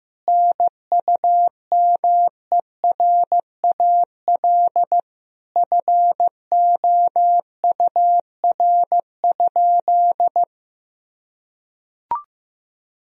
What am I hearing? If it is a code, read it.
Morse code, heard as a keyed tone, about 15 wpm, 700 Hz. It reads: NUMERAL FOUR?
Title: Sentences from Top 1000 Words - Review Part 2 - 15wpm